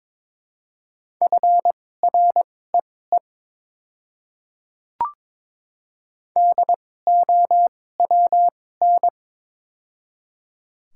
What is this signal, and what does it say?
Morse code, heard as a keyed tone, about 22 wpm, spaced as for 16 wpm, 700 Hz. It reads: FREE DOWN
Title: 2x Character Spacing - Top 500 Words - 22wpm